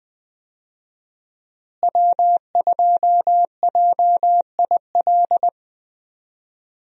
Morse code, heard as a keyed tone, about 20 wpm, 700 Hz. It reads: W2JIL
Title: USA Call Signs - 5 Characters - Repeated 3x - 20wpm